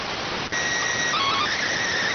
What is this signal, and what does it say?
SSTV (slow-scan television), PD 290.
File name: findings / MR AXIAL 33,99mm 1.gen-28 rx